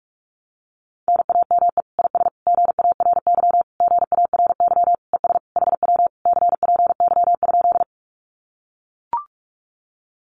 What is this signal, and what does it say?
Morse code, heard as a keyed tone, about 44 wpm, 700 Hz. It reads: NAME IS GARY GARY ES HW CPY?
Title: Common QSO Phrases Encore 45wpm